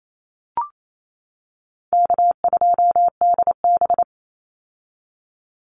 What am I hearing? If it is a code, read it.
Morse code, heard as a keyed tone, about 28 wpm, 700 Hz. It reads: K2DB
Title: Easy Call Signs - 28wpm